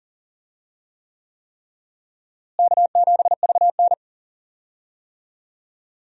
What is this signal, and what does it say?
Morse code, heard as a keyed tone, about 40 wpm, 700 Hz. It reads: K7VN
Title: USA Call Signs - 4 Characters - Repeated 3x - 40wpm